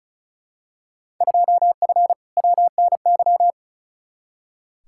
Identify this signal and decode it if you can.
Morse code, heard as a keyed tone, about 36 wpm, 690 Hz. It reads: 2F WNY
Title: Field Day Exchanges Encore - 35wpm